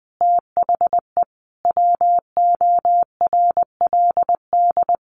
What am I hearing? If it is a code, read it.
Morse code, heard as a keyed tone, about 20 wpm, 700 Hz. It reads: THE WORLD
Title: Sentences from Top 200 Words 20wpm